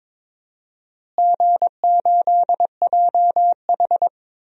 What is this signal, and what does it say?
Morse code, heard as a keyed tone, about 23 wpm, 700 Hz. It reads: G8JH
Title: DXCC-100 Call Signs - 4 Characters - Repeated 3x - 22wpm